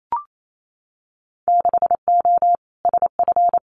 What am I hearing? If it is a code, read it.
Morse code, heard as a keyed tone, about 28 wpm, 700 Hz. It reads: 6O SF
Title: Winter Field Day Exchanges Encore 28wpm